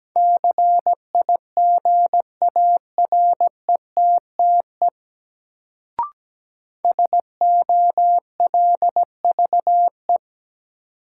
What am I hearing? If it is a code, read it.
Morse code, heard as a keyed tone, about 17 wpm, 700 Hz. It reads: CIGARETTE SOLVE
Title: Sentences from Top 1000 Words - New Word Drill - 17wpm